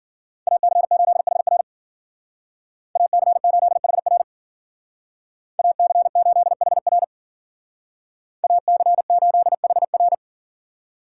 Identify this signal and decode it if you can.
Morse code, heard as a keyed tone, about 52 wpm, 700 Hz. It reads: AC8HR AC8HR AC8HR AC8HR